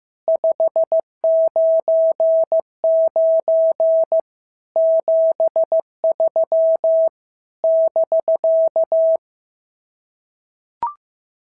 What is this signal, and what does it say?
Morse code, heard as a keyed tone, about 15 wpm, 650 Hz. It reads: 599 73 <BK>
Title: SOTA QSO Practice 15wpm